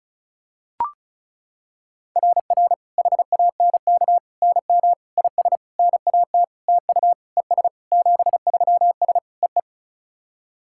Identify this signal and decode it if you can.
Morse code, heard as a keyed tone, about 34 wpm, 690 Hz. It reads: RR HANK NM IS NAT TU ES 73S EE
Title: SOTA QSO Practice 35wpm